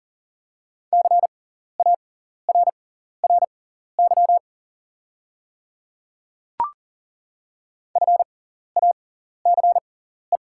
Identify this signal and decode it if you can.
Morse code, heard as a keyed tone, about 38 wpm, spaced as for 14 wpm, 690 Hz. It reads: CARRY FACE
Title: 6x Character Spacing - Top 300 Words - 40wpm